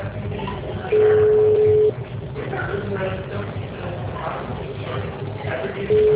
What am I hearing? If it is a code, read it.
Ringback tone (Europe, 425 Hz), with speech.
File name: out- -2014- - - .39763